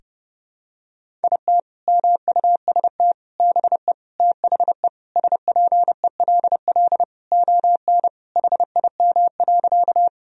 Morse code, heard as a keyed tone, about 30 wpm, 700 Hz. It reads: IT MUST BE THE SPELL ON HIM.
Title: Sentences from Top 1000 Words - Review Part 1 - 30wpm